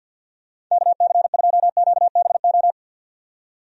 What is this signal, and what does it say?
Morse code, heard as a keyed tone, about 52 wpm, 700 Hz. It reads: KC2QBO
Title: USA Call Signs - 6 Characters - 50wpm